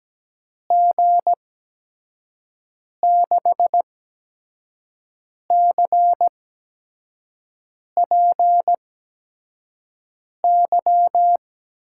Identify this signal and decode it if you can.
Morse code, heard as a keyed tone, about 17 wpm, spaced as for 5 wpm, 700 Hz. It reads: G6CPY
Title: USA Call Signs - 6 Characters - 8x Character Spacing - 17wpm